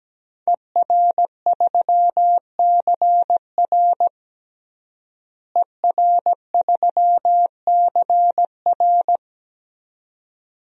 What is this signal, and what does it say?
Morse code, heard as a keyed tone, about 17 wpm, 700 Hz. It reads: ER3CR ER3CR
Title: DXCC-100 Call Signs - 5 Characters - Repeated 2x - 17wpm